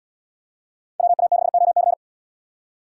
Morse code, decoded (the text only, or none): WA6OC